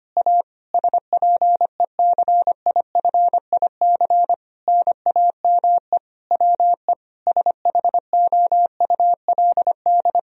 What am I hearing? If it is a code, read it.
Morse code, heard as a keyed tone, about 25 wpm, 700 Hz. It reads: A SPECIFIC NAME WE SHOULD
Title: Sentences from Top 1000 Words - Review Part 3 - 25wpm